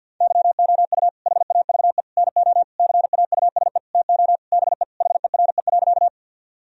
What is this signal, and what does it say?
Morse code, heard as a keyed tone, about 52 wpm, 700 Hz. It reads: YOU HAVE NO CAUSE TO BE HERE.
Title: Sentences from Top 1000 Words - Review Part 1 - 50wpm